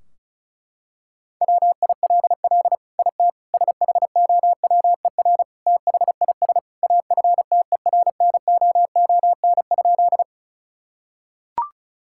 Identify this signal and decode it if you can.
Morse code, heard as a keyed tone, about 34 wpm, 700 Hz. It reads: WILL IT SHOWER THIS AFTERNOON?
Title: Sentences from Top 1600 Words 35wpm